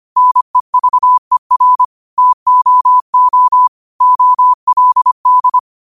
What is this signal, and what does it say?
Morse code, heard as a keyed tone, about 25 wpm, 1000 Hz. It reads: NEVER TOO OLD
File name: qotdfiles / 25wpm